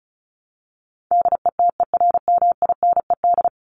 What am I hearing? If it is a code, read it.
Morse code, heard as a keyed tone, about 34 wpm, 700 Hz. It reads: DETERMINED
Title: ED Suffix - Top 100 Words with Base Words 35wpm